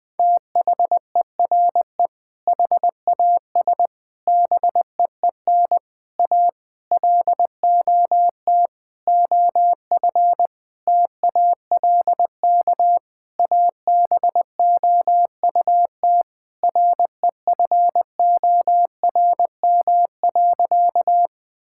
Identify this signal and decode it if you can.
Morse code, heard as a keyed tone, about 21 wpm, 700 Hz. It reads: THERE HAS BEEN A LOT OF TALK ABOUT REFORM.
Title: Sentences with 7-Word Phrases - 20wpm